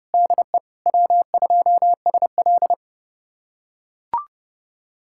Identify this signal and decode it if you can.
Morse code, heard as a keyed tone, about 30 wpm, 700 Hz. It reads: DE W2SL
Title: USA Callsign DE Callsign 30wpm